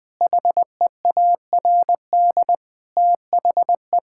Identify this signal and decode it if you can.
Morse code, heard as a keyed tone, about 20 wpm, 700 Hz. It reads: HEARD THE
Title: Sentences from Top 1600 Words 20wpm